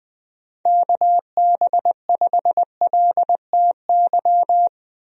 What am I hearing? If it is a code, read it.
Morse code, heard as a keyed tone, about 20 wpm, 700 Hz. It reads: KB5LTY
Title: USA Call Signs - 6 Characters - 20wpm